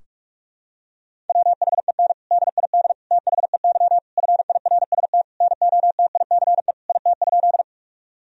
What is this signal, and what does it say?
Morse code, heard as a keyed tone, about 41 wpm, 700 Hz. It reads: WHEN DID THEY FIRST NOTICE IT?
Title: Sentences from Top 1000 Words - Review Part 2 - 45wpm